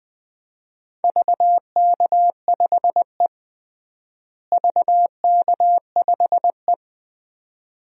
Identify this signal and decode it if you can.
Morse code, heard as a keyed tone, about 20 wpm, 700 Hz. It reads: VK5E VK5E
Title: DXCC-100 Call Signs - 4 Characters - Repeated 2x - 20wpm